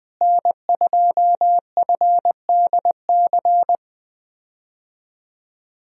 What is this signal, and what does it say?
Morse code, heard as a keyed tone, about 20 wpm, 700 Hz. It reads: N2FDC
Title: USA Call Signs - 5 Characters - Repeated 2x - 20wpm